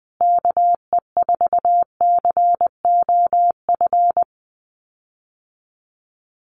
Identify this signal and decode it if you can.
Morse code, heard as a keyed tone, about 20 wpm, 700 Hz. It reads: KE4COF
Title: USA Call Signs Repeated 3x 20wpm